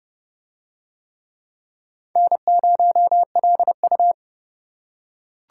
Morse code, heard as a keyed tone, about 29 wpm, 700 Hz. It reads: N0LU